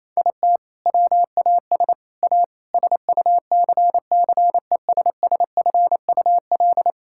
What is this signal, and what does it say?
Morse code, heard as a keyed tone, about 28 wpm, 700 Hz. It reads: IT WAS A SUCCESSFUL
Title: Sentences from Top 1000 Words - Review Part 6 - 28wpm